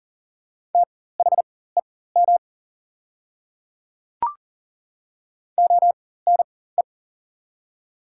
Morse code, heard as a keyed tone, about 40 wpm, spaced as for 19 wpm, 700 Hz. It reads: THEM ONE
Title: 4x Character Spacing - Top 300 Words - 40wpm